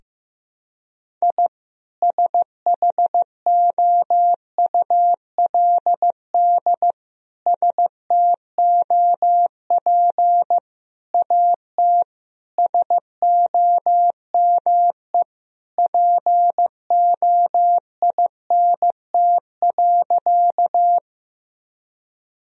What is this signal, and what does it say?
Morse code, heard as a keyed tone, about 15 wpm, 700 Hz. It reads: I SHOULD STOP AT SOME POINT.
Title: Sentences from Top 1000 Words - Review Part 1 - 15wpm